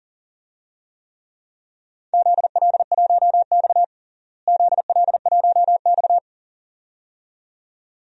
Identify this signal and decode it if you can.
Morse code, heard as a keyed tone, about 37 wpm, 700 Hz. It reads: ZL1X ZL1X
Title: DXCC-100 Call Signs - 4 Characters - Repeated 2x - 40wpm